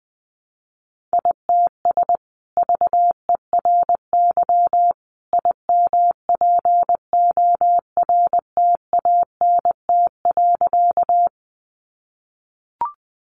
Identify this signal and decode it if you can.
Morse code, heard as a keyed tone, about 20 wpm, 700 Hz. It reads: ITS VERY IMPORTANT.
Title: Sentences from Top 600 Words 20wpm